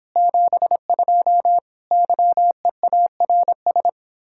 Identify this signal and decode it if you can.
Morse code, heard as a keyed tone, about 26 wpm, 700 Hz. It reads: 72 YEARS